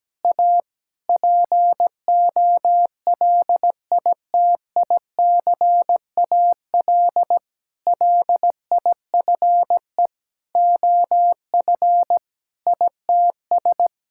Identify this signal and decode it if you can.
Morse code, heard as a keyed tone, about 17 wpm, 700 Hz. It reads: A POLITICAL LIFE OF ITS
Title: Sentences from Top 1000 Words - Review Part 5 - 17wpm